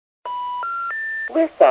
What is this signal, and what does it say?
Special information tone, with speech.